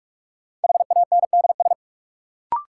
Morse code, heard as a keyed tone, about 47 wpm, 700 Hz. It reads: HANDS
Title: Sets of 4 Words - New Word Drill - 45wpm